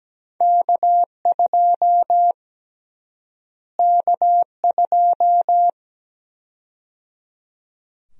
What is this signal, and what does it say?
Morse code, heard as a keyed tone, about 17 wpm, 700 Hz. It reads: K2 K2
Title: USA Call Sign Prefixes - Repeated 2x - 17wpm